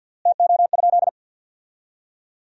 Morse code, heard as a keyed tone, about 51 wpm, 700 Hz. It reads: TO?
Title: 3x Word-Spacing - Sentences from Top 500 Words - 50wpm